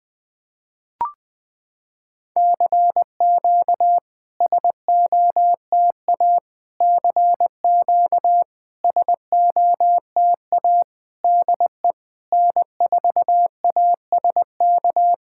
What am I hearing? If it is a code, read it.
Morse code, heard as a keyed tone, about 20 wpm, 690 Hz. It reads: CQ SOTA CQ SOTA DE N4ASK
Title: SOTA QSO Practice 20wpm